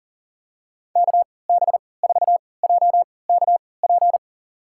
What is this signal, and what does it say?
Morse code, heard as a keyed tone, about 37 wpm, spaced as for 22 wpm, 700 Hz. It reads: KB4JKP
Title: USA Call Signs - 6 Characters - 3x Character Spacing - 40wpm